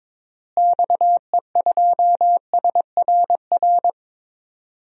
Morse code, heard as a keyed tone, about 22 wpm, 700 Hz. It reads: XE2SRR
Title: DXCC-100 Call Signs - 6 Characters - Repeated 3x - 22wpm